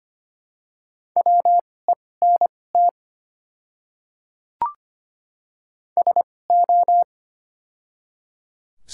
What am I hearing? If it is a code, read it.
Morse code, heard as a keyed tone, about 25 wpm, spaced as for 18 wpm, 700 Hz. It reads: WENT SO
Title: 2x Character Spacing - Top 300 Words - 25wpm